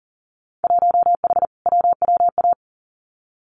Morse code, heard as a keyed tone, about 42 wpm, 700 Hz. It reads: 1H WWA
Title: Winter Field Day Exchanges 40wpm